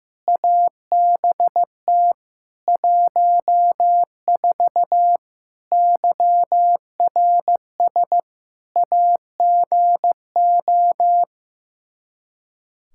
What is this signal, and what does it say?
Morse code, heard as a keyed tone, about 16 wpm, 700 Hz. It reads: ABT 14 YRS AGO
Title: Practice QSO - Exchange 3 - More Info - 15wpm